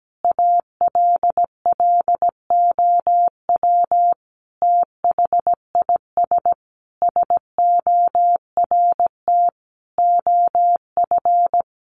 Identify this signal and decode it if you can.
Morse code, heard as a keyed tone, about 17 wpm, 700 Hz. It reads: ALLOW THIS SORT OF
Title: Sentences from Top 1300 Words 17wpm